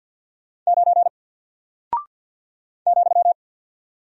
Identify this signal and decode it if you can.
Morse code, heard as a keyed tone, about 48 wpm, 700 Hz. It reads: 9 ,
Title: Single Character (Letter:Number:Punctuation:Prosign) 50wpm